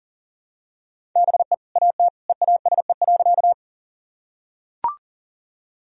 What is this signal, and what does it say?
Morse code, heard as a keyed tone, about 40 wpm, 700 Hz. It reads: BE AT EASE.
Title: Sentences from Top 1000 Words - Review Part 1 - 40wpm